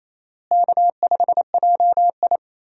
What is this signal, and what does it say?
Morse code, heard as a keyed tone, about 28 wpm, 700 Hz. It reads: K5JI
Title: USA Call Signs - 4 Characters - Repeated 3x - 28wpm